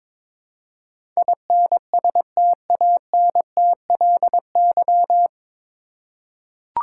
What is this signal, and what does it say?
Morse code, heard as a keyed tone, about 22 wpm, 700 Hz. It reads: INSTANTLY
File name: 9 Letter Words - First 3 Letters Given - 22wpm